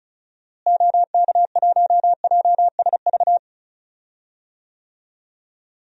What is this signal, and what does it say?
Morse code, heard as a keyed tone, about 34 wpm, 700 Hz. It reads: OK1JSV
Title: DXCC-100 Call Signs - 6 Characters - Repeated 2x - 35wpm